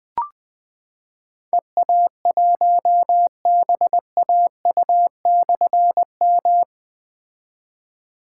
Morse code, heard as a keyed tone, about 20 wpm, 700 Hz. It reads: EA1BAU/M
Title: Difficult Call Signs 20wpm